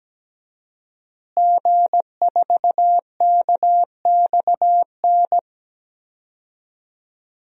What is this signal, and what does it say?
Morse code, heard as a keyed tone, about 17 wpm, 700 Hz. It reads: G4KXN
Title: DXCC-100 Call Signs - 5 Characters - Repeated 2x - 17wpm